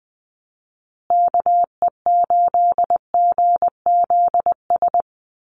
Morse code, heard as a keyed tone, about 20 wpm, 700 Hz. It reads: KE8GZS
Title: USA Call Signs Repeated 3x 20wpm